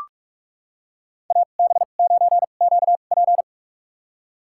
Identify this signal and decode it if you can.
Morse code, heard as a keyed tone, about 46 wpm, spaced as for 33 wpm, 700 Hz. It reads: AB9QP
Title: USA Call Signs - 5 Characters - 2x Character Spacing - 45wpm